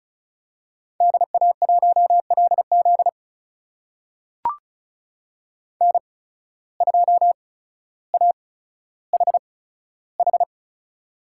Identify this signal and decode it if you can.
Morse code, heard as a keyed tone, about 34 wpm, spaced as for 15 wpm, 700 Hz. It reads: DA1LZ N2AHH
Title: USA Call Signs - 5 Characters - 8x Character Spacing - 35wpm